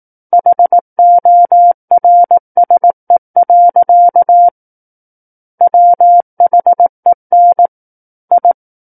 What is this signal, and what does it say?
Morse code, heard as a keyed tone, about 18 wpm, 700 Hz. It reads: HORSE. WHEN I
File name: Book One / 09-Chapter 9 - 18 WPM